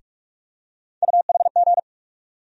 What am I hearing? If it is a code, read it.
Morse code, heard as a keyed tone, about 44 wpm, 700 Hz. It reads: UHG
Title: USA Call Sign Postfixes - 3 Letters 45wpm